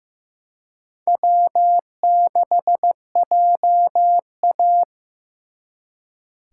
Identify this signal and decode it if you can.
Morse code, heard as a keyed tone, about 18 wpm, 700 Hz. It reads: W6JA
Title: Speed Racing - USA 1x2 Call Signs 15wpm